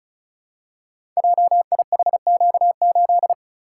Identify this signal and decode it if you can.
Morse code, heard as a keyed tone, about 36 wpm, 700 Hz. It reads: JIHQ8
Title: Code Groups - 5 Letter:Numbers - 35wpm